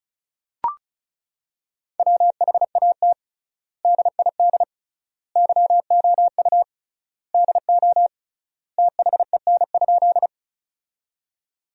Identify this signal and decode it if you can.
Morse code, heard as a keyed tone, about 34 wpm, 700 Hz. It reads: WHAT DID YOU DO THEN?
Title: 3x Word-Spacing - Sentences from Top 100 Words - 35wpm